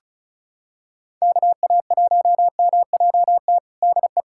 Morse code, heard as a keyed tone, about 36 wpm, 700 Hz. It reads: KA1MJT DE
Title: USA Callsign DE Callsign - 6 Characters - 35wpm